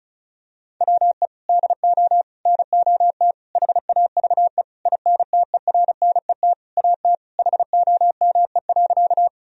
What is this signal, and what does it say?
Morse code, heard as a keyed tone, about 36 wpm, 700 Hz. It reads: WE DO NOT HAVE INTERNET AT HOME.
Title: Sentences from Top 1000 Words - Review Part 4 - 35wpm